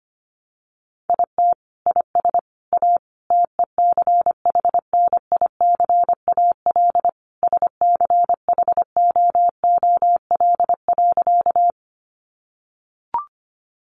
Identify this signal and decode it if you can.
Morse code, heard as a keyed tone, about 25 wpm, 700 Hz. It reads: IT IS A TECHNICAL SCHOOL.
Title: Sentences from Top 1100 Words 25wpm